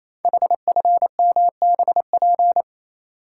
Morse code, heard as a keyed tone, about 27 wpm, 700 Hz. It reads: HFMBP
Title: Code Groups - 5 Letter:Numbers - 28wpm